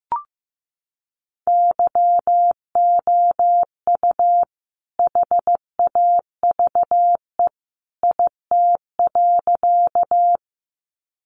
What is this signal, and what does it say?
Morse code, heard as a keyed tone, about 14 wpm, 700 Hz. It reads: YOU HAVE IT.
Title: Sentences from Top 100 Words 15wpm